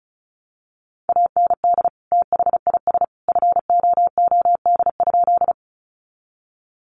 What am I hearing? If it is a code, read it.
Morse code, heard as a keyed tone, about 34 wpm, 700 Hz. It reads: AND THIS FOOD?
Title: Sentences from Top 300 Words 35wpm